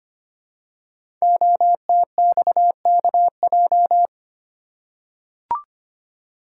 Morse code, heard as a keyed tone, about 25 wpm, 700 Hz. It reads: OTXKJ
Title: Code Groups - 5 Letter:Numbers - 25wpm